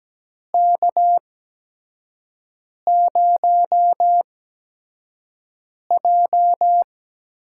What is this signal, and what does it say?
Morse code, heard as a keyed tone, about 17 wpm, spaced as for 5 wpm, 700 Hz. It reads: K0J
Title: USA Call Signs - 4 Characters - 8x Character Spacing - 17wpm